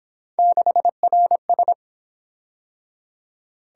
Morse code, heard as a keyed tone, about 27 wpm, 700 Hz. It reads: 6RS
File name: Speed-Racing - Three Letter-Number Combinations - 22wpm